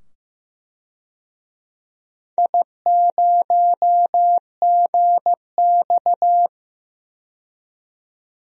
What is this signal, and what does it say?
Morse code, heard as a keyed tone, about 15 wpm, 700 Hz. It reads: I0GX
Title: DXCC-100 Call Signs - 4 Characters - 15wpm